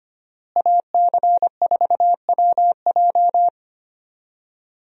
Morse code, heard as a keyed tone, about 25 wpm, 700 Hz. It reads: AC4WJ
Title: USA Call Signs - 5 Characters - Repeated 2x - 25wpm